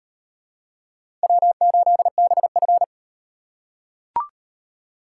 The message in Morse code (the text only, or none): W8BF